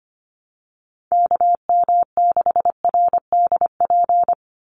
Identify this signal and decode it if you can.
Morse code, heard as a keyed tone, about 25 wpm, 700 Hz. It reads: KM6RDP
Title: USA Call Signs 25wpm